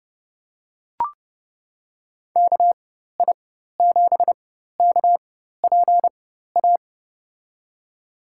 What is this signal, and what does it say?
Morse code, heard as a keyed tone, about 30 wpm, spaced as for 14 wpm, 700 Hz. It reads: KI7KPA
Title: USA Call Signs - 6 Characters - 4x Character Spacing - 30wpm